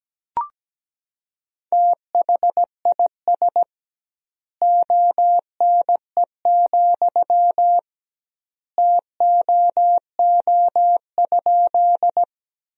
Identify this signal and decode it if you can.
Morse code, heard as a keyed tone, about 17 wpm, 700 Hz. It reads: THIS ONE, TOO?
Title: 2x Word-Spacing - Sentences from Top 200 Words - 17wpm